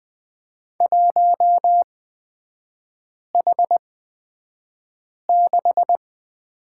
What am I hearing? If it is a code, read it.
Morse code, heard as a keyed tone, about 20 wpm, spaced as for 5 wpm, 700 Hz. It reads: 1H6
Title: 250 Lesson 40e - BK - Three Characters 20wpm